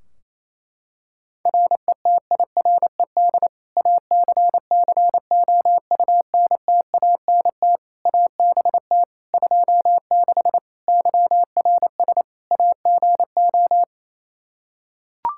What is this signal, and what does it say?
Morse code, heard as a keyed tone, about 28 wpm, 700 Hz. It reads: RETIRED ACCOUNTANT ABT 26 YRS AGO
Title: Practice QSO - Exchange 3 - More Info - 28wpm